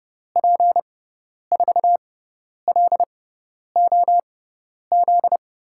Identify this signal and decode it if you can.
Morse code, heard as a keyed tone, about 30 wpm, spaced as for 10 wpm, 700 Hz. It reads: P4LOZ
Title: USA Call Signs - 6 Characters - 6x Character Spacing - 30wpm